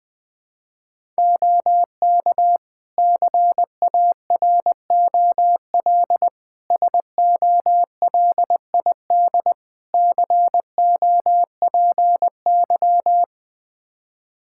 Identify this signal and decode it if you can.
Morse code, heard as a keyed tone, about 20 wpm, 700 Hz. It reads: OK CAROL SOLID COPY
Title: K1USN SST QSO Practice 20wpm